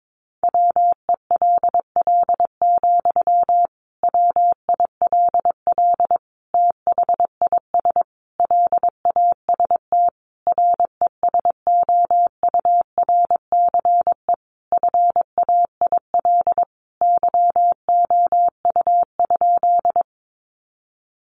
Morse code, heard as a keyed tone, about 22 wpm, 700 Hz. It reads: WELL, WILL THIS LAST RESOURCE FAIL YOU?
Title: Sentences from Top 1300 Words 22wpm